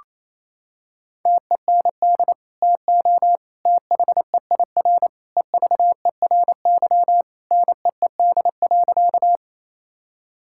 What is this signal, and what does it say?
Morse code, heard as a keyed tone, about 27 wpm, 700 Hz. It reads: TEND TO THEIR EVERY NEED.
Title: Sentences from Top 1000 Words - Review Part 5 - 28wpm